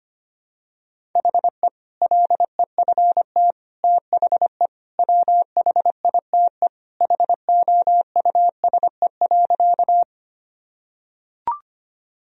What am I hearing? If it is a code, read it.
Morse code, heard as a keyed tone, about 25 wpm, 700 Hz. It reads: HE LEFT THE WHITE HOUSE.
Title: Sentences from Top 1000 Words - Review Part 1 - 25wpm